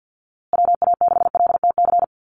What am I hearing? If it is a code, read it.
Morse code, heard as a keyed tone, about 50 wpm, 700 Hz. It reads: WA6LTC
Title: USA Call Signs Repeated 3x 50wpm